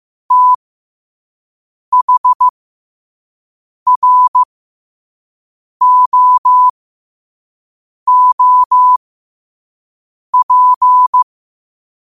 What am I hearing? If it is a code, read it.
Morse code, heard as a keyed tone, about 15 wpm, spaced as for 5 wpm, 1000 Hz. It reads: THROOP
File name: qotdfiles / 5wpm